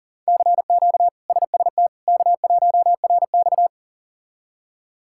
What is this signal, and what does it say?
Morse code, heard as a keyed tone, about 40 wpm, 700 Hz. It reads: CQ SST K1RX
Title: K1USN SST QSO Practice - 40wpm